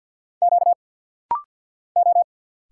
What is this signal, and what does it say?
Morse code, heard as a keyed tone, about 46 wpm, 690 Hz. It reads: Q O
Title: Single Letters - Rapid-Fire 50wpm